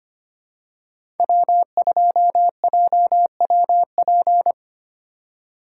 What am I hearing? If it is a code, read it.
Morse code, heard as a keyed tone, about 25 wpm, 700 Hz. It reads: W2JWP
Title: USA Call Signs - 5 Characters - 2x Character Spacing - 25wpm